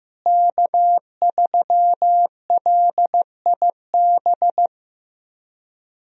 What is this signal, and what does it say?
Morse code, heard as a keyed tone, about 15 wpm, 700 Hz. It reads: K3LIB
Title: USA Call Signs - 5 Characters - Repeated 3x - 15wpm